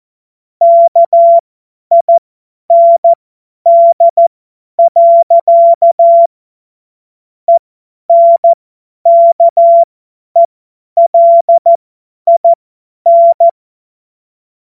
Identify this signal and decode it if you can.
Morse code, heard as a keyed tone, about 14 wpm, spaced as for 10 wpm, 680 Hz. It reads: KIND. ENKELIN